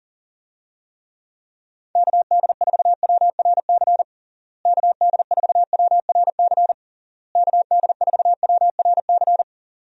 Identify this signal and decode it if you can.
Morse code, heard as a keyed tone, about 37 wpm, 700 Hz. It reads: KD4WRC KD4WRC KD4WRC
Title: USA Call Signs - 6 Characters - Repeated 3x - 40wpm